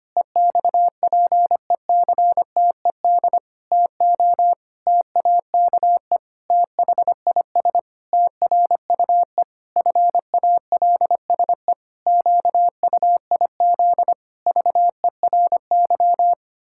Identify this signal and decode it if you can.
Morse code, heard as a keyed tone, about 25 wpm, 700 Hz. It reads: EXPECTED TO TAKE THIS TRUE FALSE QUIZ VERY
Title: Pangram Challenge v2 - 25wpm